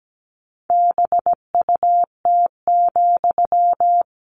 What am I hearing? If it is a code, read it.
Morse code, heard as a keyed tone, about 17 wpm, 700 Hz. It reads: BUT,
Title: Sentences from Top 600 Words 17wpm